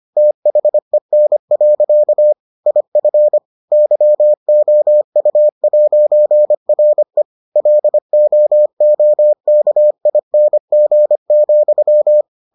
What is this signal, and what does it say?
Morse code, heard as a keyed tone, about 25 wpm, 600 Hz. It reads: THEN. IF YOU'RE LOOKING,